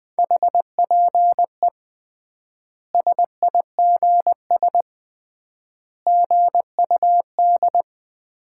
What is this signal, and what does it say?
Morse code, heard as a keyed tone, about 21 wpm, 700 Hz. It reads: HPE SIGS GUD
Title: Common QSO Phrases - 3x Word-Spacing - 20wpm